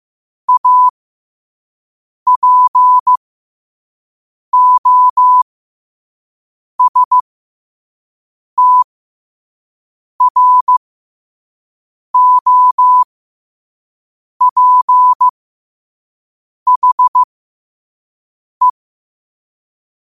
Morse code, heard as a keyed tone, about 15 wpm, spaced as for 5 wpm, 1000 Hz. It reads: APOSTROPHE
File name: qotdfiles / 5wpm